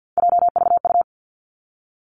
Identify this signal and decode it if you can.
Morse code, heard as a keyed tone, about 50 wpm, 700 Hz. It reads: JVU
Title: Three Challenging Letter-Number Combinations 50wpm